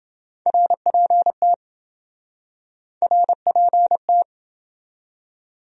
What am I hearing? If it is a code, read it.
Morse code, heard as a keyed tone, about 27 wpm, 700 Hz. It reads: RPT RPT